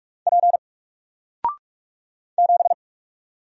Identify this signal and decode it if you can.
Morse code, heard as a keyed tone, about 46 wpm, 700 Hz. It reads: P 7
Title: Single Character (Letter:Number:Punctuation:Prosign) 45wpm